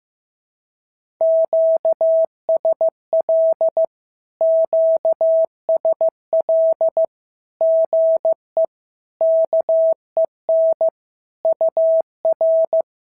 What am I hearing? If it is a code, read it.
Morse code, heard as a keyed tone, about 15 wpm, 650 Hz. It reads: QSL QSL GE KEN UR
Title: SOTA QSO Practice 15wpm